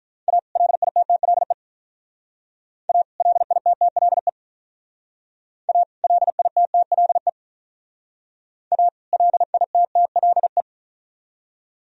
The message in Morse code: A LITTLE A LITTLE A LITTLE A LITTLE

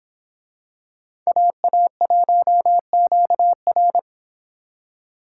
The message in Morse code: AA1QR

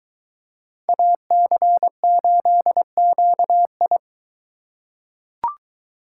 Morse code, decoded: AC8QI

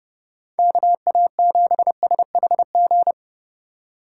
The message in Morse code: KA7SHG

